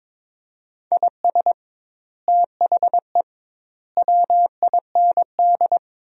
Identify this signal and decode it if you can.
Morse code, heard as a keyed tone, about 22 wpm, 700 Hz. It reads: IS THE WIND
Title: 2x Word-Spacing - Sentences from Top 400 Words - 22wpm